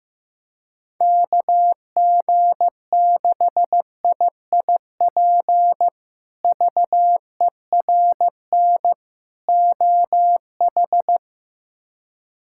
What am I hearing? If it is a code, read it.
Morse code, heard as a keyed tone, about 15 wpm, 700 Hz. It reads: KG6IIP VERN OH
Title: K1USN SST QSO Practice - 15wpm